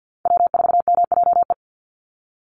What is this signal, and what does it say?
Morse code, heard as a keyed tone, about 50 wpm, 700 Hz. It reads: W4MJE
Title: Call Signs Repeated 3x - 50wpm